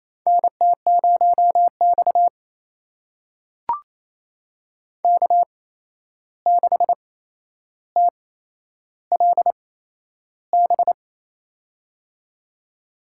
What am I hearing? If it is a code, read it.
Morse code, heard as a keyed tone, about 28 wpm, spaced as for 11 wpm, 700 Hz. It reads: NT0X K6TLB
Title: USA Call Signs - 8x Character Spacing - 28wpm